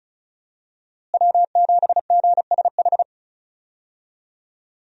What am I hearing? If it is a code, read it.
Morse code, heard as a keyed tone, about 34 wpm, 700 Hz. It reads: W7GSH
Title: USA Call Signs - 5 Characters - Repeated 2x - 35wpm